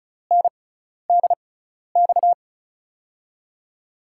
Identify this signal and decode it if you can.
Morse code, heard as a keyed tone, about 36 wpm, spaced as for 12 wpm, 700 Hz. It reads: NDX
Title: USA Call Signs - 6x Character Spacing - 35wpm